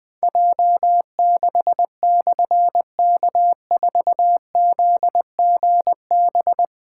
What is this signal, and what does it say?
Morse code, heard as a keyed tone, about 20 wpm, 700 Hz. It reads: J6/K4ZGB
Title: Difficult Call Signs 20wpm